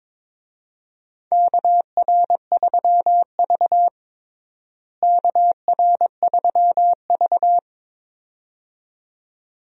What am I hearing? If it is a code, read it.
Morse code, heard as a keyed tone, about 22 wpm, 700 Hz. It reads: KR3V KR3V
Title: USA Call Signs - 4 Characters - Repeated 2x - 22wpm